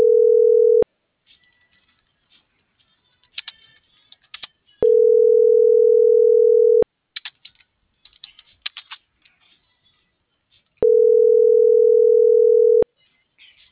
Ringback tone (North America), 440 + 480 Hz.